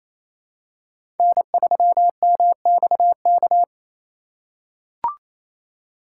Morse code, heard as a keyed tone, about 28 wpm, 700 Hz. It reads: N3MXK
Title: USA Call Signs - 5 Characters - 2x Character Spacing - 28wpm